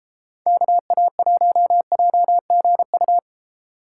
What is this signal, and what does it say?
Morse code, heard as a keyed tone, about 30 wpm, 700 Hz. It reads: KA1JGU